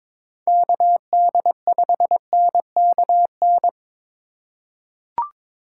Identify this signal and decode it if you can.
Morse code, heard as a keyed tone, about 21 wpm, 700 Hz. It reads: KD5NKN
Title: USA Call Signs - 6 Characters - 2x Character Spacing - 22wpm